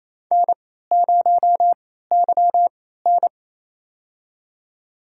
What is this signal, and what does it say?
Morse code, heard as a keyed tone, about 28 wpm, spaced as for 16 wpm, 700 Hz. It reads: N0YN
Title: USA Call Signs - 4 Characters - 3x Character Spacing - 28wpm